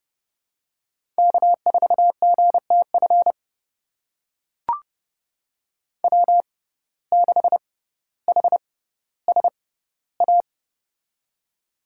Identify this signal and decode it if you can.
Morse code, heard as a keyed tone, about 30 wpm, spaced as for 15 wpm, 700 Hz. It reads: K4GTF W6HSA
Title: USA Call Signs - 5 Characters - 6x Character Spacing - 30wpm